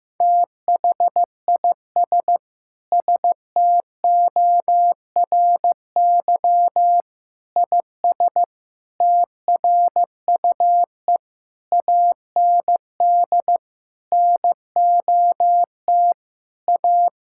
Morse code, heard as a keyed tone, about 15 wpm, 700 Hz. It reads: THIS STORY IS TRUE AND NOT A
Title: Sentences from Top 1600 Words 15wpm